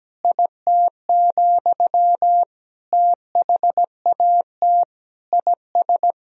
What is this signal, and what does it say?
Morse code, heard as a keyed tone, about 17 wpm, 700 Hz. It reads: IT, THAT IS
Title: Sentences from Top 400 Words and Abbreviations 17wpm